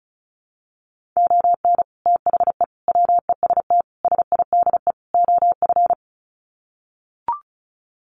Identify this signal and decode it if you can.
Morse code, heard as a keyed tone, about 34 wpm, 700 Hz. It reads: ON THE WEST SIDE OF
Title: Sets of 5 Words Encore2 35wpm